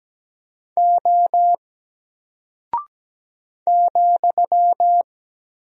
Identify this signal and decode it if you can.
Morse code, heard as a keyed tone, about 17 wpm, 700 Hz. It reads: O ,
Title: Single Character (Letter:Number:Punctuation:Prosign) Alt 17wpm